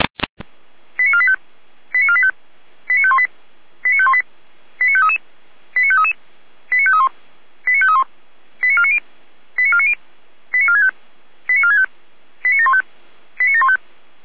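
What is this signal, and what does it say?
Five-tone selective call (ZVEI), heard as quick runs of five tones, tones of about 70 ms. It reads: 89485 89485 89518 89518 89533 89533 89531 89531 89490 89490 89465 89465 89715 89715
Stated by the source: ZVEI five-tone selective call 89462 received by SDR.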